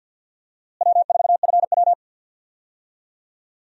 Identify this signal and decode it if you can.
Morse code, heard as a keyed tone, about 50 wpm, 690 Hz. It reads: W4FW
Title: USA Call Signs - 4 Characters - Repeated 2x - 50wpm